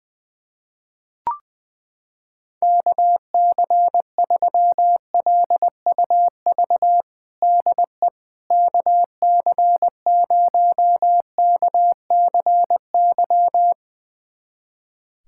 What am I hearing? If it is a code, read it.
Morse code, heard as a keyed tone, about 20 wpm, 700 Hz. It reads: KC3LUV DE KC0KCY
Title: USA Callsign DE Callsign 20wpm